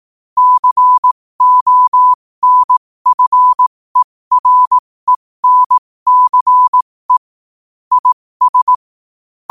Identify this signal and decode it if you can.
Morse code, heard as a keyed tone, about 18 wpm, spaced as for 15 wpm, 1000 Hz. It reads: CONFERENCE IS
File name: qotdfiles / 15wpm